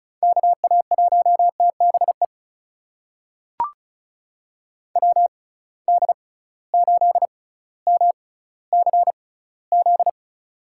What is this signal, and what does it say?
Morse code, heard as a keyed tone, about 36 wpm, spaced as for 18 wpm, 700 Hz. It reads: KA1TBE WD8MCZ
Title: USA Call Signs - 6 Characters - 6x Character Spacing - 35wpm